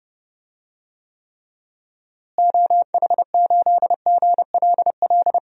Morse code, heard as a keyed tone, about 30 wpm, 700 Hz. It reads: OH8GLL
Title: DXCC-100 Call Signs - 6 Characters - Repeated 3x - 30wpm